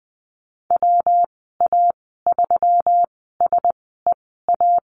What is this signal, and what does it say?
Morse code, heard as a keyed tone, about 19 wpm, spaced as for 14 wpm, 700 Hz. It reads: WA3SEA
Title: 122 Lesson 20g - Number 3 - Call Signs 20wpm